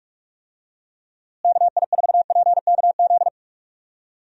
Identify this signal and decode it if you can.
Morse code, heard as a keyed tone, about 46 wpm, 700 Hz. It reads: KI4PKZ
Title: USA Call Signs - 6 Characters - Repeated 3x - 45wpm